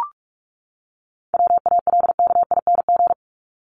Morse code, heard as a keyed tone, about 44 wpm, 700 Hz. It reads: WALKING
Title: ING Suffix - Top 100 Words with Base Words 45wpm